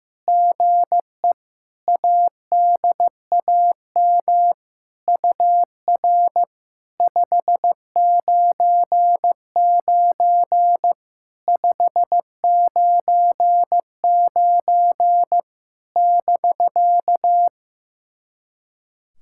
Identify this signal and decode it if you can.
Morse code, heard as a keyed tone, about 16 wpm, 700 Hz. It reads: GE ADAM UR 599 599 <BK>
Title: SOTA QSO Practice 15wpm